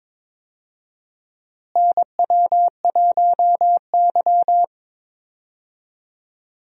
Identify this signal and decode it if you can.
Morse code, heard as a keyed tone, about 21 wpm, 700 Hz. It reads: NW1Y
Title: USA Call Signs - 4 Characters - Repeated 2x - 22wpm